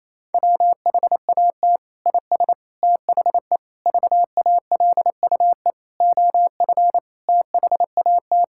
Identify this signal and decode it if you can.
Morse code, heard as a keyed tone, about 28 wpm, 700 Hz. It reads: WHAT IS THE VALUE OF THAT
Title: Sentences from Top 1000 Words - Review Part 6 - 28wpm